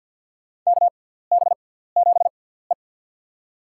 Morse code, heard as a keyed tone, about 53 wpm, spaced as for 17 wpm, 690 Hz. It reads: KB7E
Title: USA Call Signs - 6x Character Spacing - 50wpm